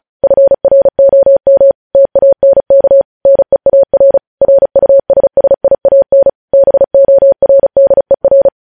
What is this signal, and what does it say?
Morse code, heard as a keyed tone, about 36 wpm, 550 Hz. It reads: FROM TANK NEAR RUSSIAN BORDER
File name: podcast / News headlines-1x-35x35